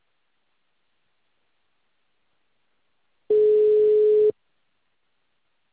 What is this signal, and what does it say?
Ringback tone (Europe, 425 Hz).